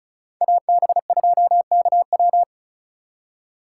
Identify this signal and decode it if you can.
Morse code, heard as a keyed tone, about 42 wpm, 700 Hz. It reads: AB2KW
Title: Speed Racing - USA 2x2 Call Signs 35wpm